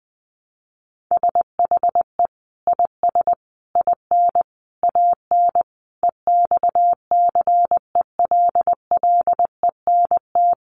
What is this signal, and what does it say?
Morse code, heard as a keyed tone, about 20 wpm, 700 Hz. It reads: SHE IS IN AN EXCELLENT
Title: Sentences from Top 1000 Words 20wpm